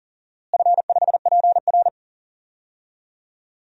Morse code, heard as a keyed tone, about 39 wpm, 700 Hz. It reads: F5PR
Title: DXCC-100 Call Signs - 4 Characters - Repeated 3x - 40wpm